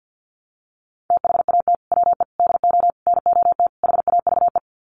Morse code, heard as a keyed tone, about 48 wpm, 700 Hz. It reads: THAT WE DO NOT HAVE
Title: Sets of 5 Words Encore 50wpm